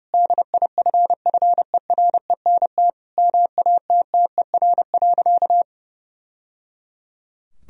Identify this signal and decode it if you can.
Morse code, heard as a keyed tone, about 32 wpm, 700 Hz. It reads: DIFFERENT MATTER.
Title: Sentences from Top 1000 Words - Review Part 3 - 30wpm